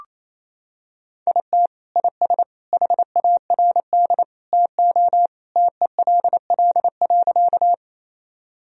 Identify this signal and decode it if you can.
Morse code, heard as a keyed tone, about 28 wpm, 700 Hz. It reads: IT IS HARD TO TELL.
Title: Sentences from Top 1000 Words - Review Part 1 - 28wpm